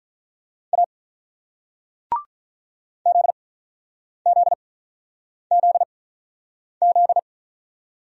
Morse code, heard as a keyed tone, about 47 wpm, spaced as for 9 wpm, 700 Hz. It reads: A ZZZZ